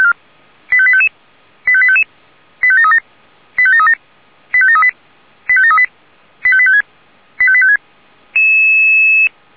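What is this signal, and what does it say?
Five-tone selective call (ZVEI), heard as quick runs of five tones, tones of about 70 ms. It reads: 85855 85855 85737 85737 85738 85738 85856 85856
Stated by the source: ZVEI five-tone selective call received on an SDR.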